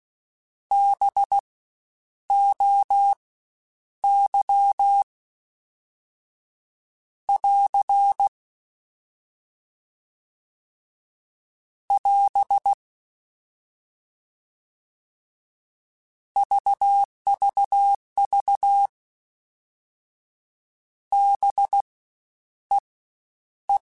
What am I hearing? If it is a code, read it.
Morse code, heard as a keyed tone, about 16 wpm, spaced as for 9 wpm, 800 Hz. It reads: BOY <AR> <AS> VVV BEE
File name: Spaced Alphas / Spaced Alphas mp3s / Bs sp @15